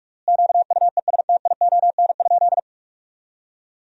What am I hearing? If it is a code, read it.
Morse code, heard as a keyed tone, about 44 wpm, 700 Hz. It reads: QUESTION?
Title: 3x Word-Spacing - Sentences from Top 400 Words - 45wpm